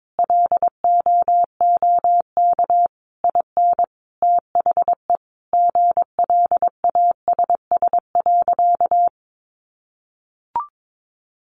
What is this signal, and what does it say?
Morse code, heard as a keyed tone, about 22 wpm, 700 Hz. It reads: LOOK IN THE GLASS.
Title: Sentences from Top 1300 Words 22wpm